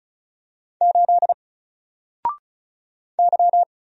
Morse code, heard as a keyed tone, about 34 wpm, 700 Hz. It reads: 8 Y
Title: Single Letter-Number Alt 35wpm